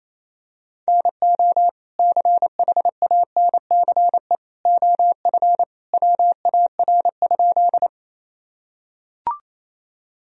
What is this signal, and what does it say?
Morse code, heard as a keyed tone, about 28 wpm, 700 Hz. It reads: NO CHANCE OF WAR?
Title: Sentences from Top 1000 Words - Review Part 2 - 28wpm